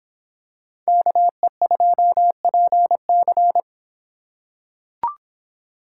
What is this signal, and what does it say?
Morse code, heard as a keyed tone, about 25 wpm, 700 Hz. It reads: KE2PC